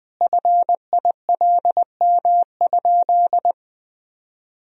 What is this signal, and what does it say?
Morse code, heard as a keyed tone, about 20 wpm, 700 Hz. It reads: FILM?
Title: Sentences from Top 1000 Words - Review Part 3 - 20wpm